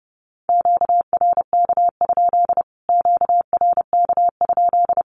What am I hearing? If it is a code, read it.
Morse code, heard as a keyed tone, about 30 wpm, 700 Hz. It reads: QRK? QRK?
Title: Q-Codes Repeated 3x 30wpm